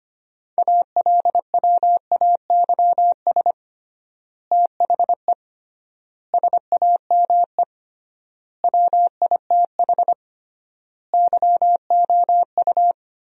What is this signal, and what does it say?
Morse code, heard as a keyed tone, about 25 wpm, 700 Hz. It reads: ALWAYS THE SAME WITH YOU
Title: 3x Word-Spacing - Sentences from Top 300 Words - 25wpm